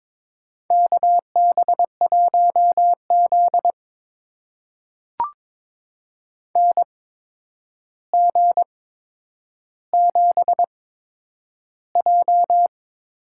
Morse code, heard as a keyed tone, about 22 wpm, spaced as for 9 wpm, 700 Hz. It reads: KB1Z NG7J
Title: USA Call Signs - 8x Character Spacing - 22wpm